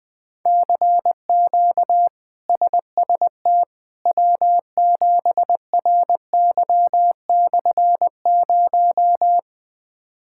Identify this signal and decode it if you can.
Morse code, heard as a keyed tone, about 20 wpm, 700 Hz. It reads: CQ SST W7RY/0
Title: K1USN SST QSO Practice - 20wpm